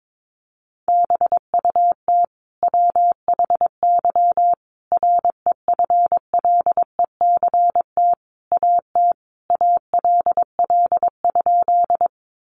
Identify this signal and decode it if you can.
Morse code, heard as a keyed tone, about 22 wpm, 700 Hz. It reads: BUT WHY REFLECT AT ALL?
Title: Sentences from Top 1500 Words 22wpm